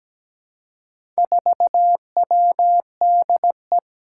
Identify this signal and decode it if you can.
Morse code, heard as a keyed tone, about 17 wpm, 700 Hz. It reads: 4WDE
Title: Four Challenging Letter-Number Combinations 17wpm